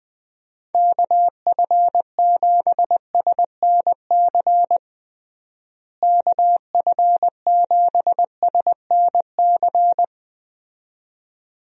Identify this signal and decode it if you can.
Morse code, heard as a keyed tone, about 20 wpm, 700 Hz. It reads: KF7SNC KF7SNC
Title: USA Call Signs - 6 Characters - Repeated 2x - 20wpm